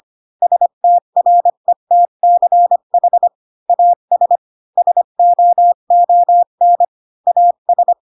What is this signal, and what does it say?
Morse code, heard as a keyed tone, about 25 wpm, 700 Hz. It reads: STRETCH AS SOON AS
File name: Pinocchio / Pinocchio ch3 20 25 ik1hge